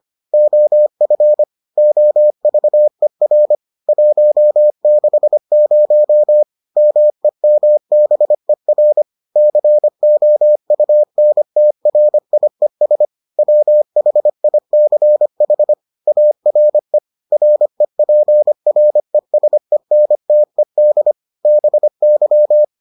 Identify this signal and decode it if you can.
Morse code, heard as a keyed tone, about 25 wpm, 600 Hz. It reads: OF OVER 160 MEMBER COUNTRIES WHICH ARE REPRESENTED BY